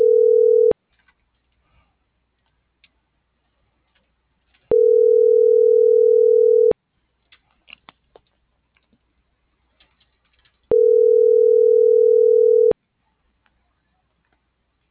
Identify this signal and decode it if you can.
Ringback tone (North America), 440 + 480 Hz.
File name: CENTRAL MAERSK / out adolfo.msk